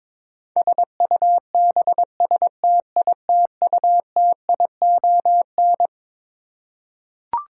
Morse code, heard as a keyed tone, about 22 wpm, 700 Hz. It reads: SUBSTITUTION
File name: Twelve Letter Words - 22wpm